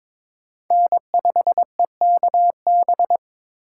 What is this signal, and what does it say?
Morse code, heard as a keyed tone, about 26 wpm, 700 Hz. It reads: N5EKB
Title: Speed Racing - USA 1x3 Call Signs 22wpm